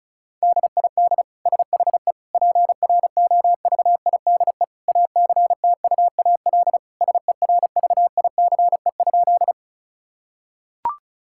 Morse code, heard as a keyed tone, about 36 wpm, 700 Hz. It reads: DID SHE PROVIDE ACTUAL SERVICE?
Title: Sentences from Top 1000 Words - Review Part 5 - 35wpm